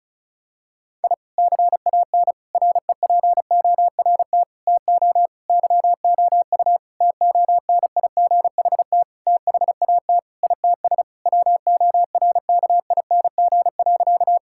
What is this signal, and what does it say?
Morse code, heard as a keyed tone, about 34 wpm, 700 Hz. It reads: I CAN REPORT TO YOU TONIGHT THAT ITS WORKING.
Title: Sentences from Top 1000 Words - Review Part 6 - 35wpm